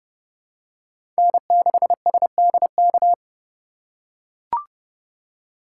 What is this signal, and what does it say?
Morse code, heard as a keyed tone, about 31 wpm, 700 Hz. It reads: N6SDK
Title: USA Call Signs - 3x Character Spacing - 30wpm